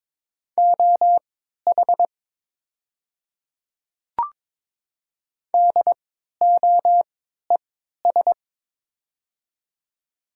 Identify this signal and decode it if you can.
Morse code, heard as a keyed tone, about 22 wpm, spaced as for 13 wpm, 700 Hz. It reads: OH DOES
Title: 3x Character Spacing - Top 500 Words - 22wpm